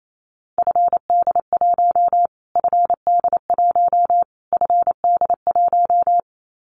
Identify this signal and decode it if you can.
Morse code, heard as a keyed tone, about 28 wpm, 700 Hz. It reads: FD1 FD1 FD1
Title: Callsign Prefixes for Top 25 DXCC - Repeated 3x - 28wpm